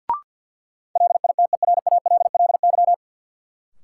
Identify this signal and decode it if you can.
Morse code, heard as a keyed tone, about 52 wpm, 700 Hz. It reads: LITERALLY
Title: Sentences from Top 1400 Words - New Word Drill - 50wpm